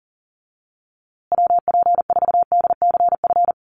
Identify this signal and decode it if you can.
Morse code, heard as a keyed tone, about 37 wpm, 700 Hz. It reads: WP4DCF
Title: USA Call Signs Repeated 2x 40wpm